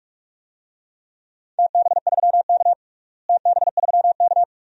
Morse code, heard as a keyed tone, about 46 wpm, 700 Hz. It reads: TB3K TB3K
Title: DXCC-100 Call Signs - 4 Characters - Repeated 2x - 45wpm